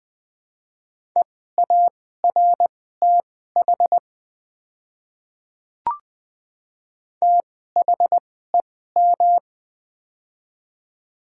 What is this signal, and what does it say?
Morse code, heard as a keyed tone, about 20 wpm, spaced as for 15 wpm, 700 Hz. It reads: EARTH THEM
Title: 2x Character Spacing - Top 200 Words - 20wpm